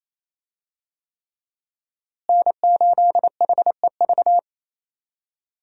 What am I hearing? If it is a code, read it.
Morse code, heard as a keyed tone, about 27 wpm, 700 Hz. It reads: N8HEV
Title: USA Call Signs - 5 Characters - Repeated 3x - 28wpm